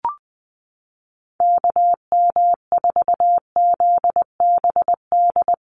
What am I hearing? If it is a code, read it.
Morse code, heard as a keyed tone, about 20 wpm, 700 Hz. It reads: KM4ZBD
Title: USA Call Signs Repeated 3x 20wpm